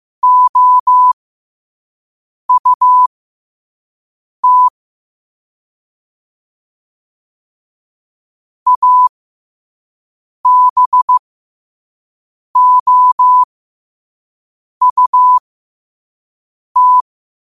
Morse code, heard as a keyed tone, about 15 wpm, spaced as for 5 wpm, 1000 Hz. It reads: OUT ABOUT
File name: qotdfiles / 5wpm